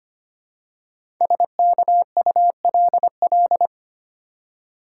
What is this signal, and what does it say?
Morse code, heard as a keyed tone, about 25 wpm, 700 Hz. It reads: SKULL
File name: 5 Letter Words - First 3 Letters Given - 25wpm